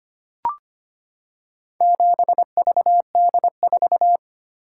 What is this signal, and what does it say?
Morse code, heard as a keyed tone, about 25 wpm, 700 Hz. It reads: 7VD4
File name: practice / 25wpm / Four Letter-Number Combinations - 25wpm